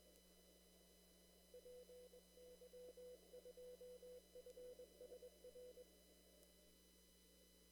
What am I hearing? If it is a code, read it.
Morse code, heard as a keyed tone, about 22 wpm, 500 Hz. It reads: PY2FSR
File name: CW / kt5j wpx cw-op-k5pi